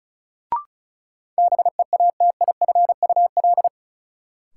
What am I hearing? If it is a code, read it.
Morse code, heard as a keyed tone, about 36 wpm, 700 Hz. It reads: BEATIFUL
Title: Sentences from Top 1300 Words - New Word Drill - 35wpm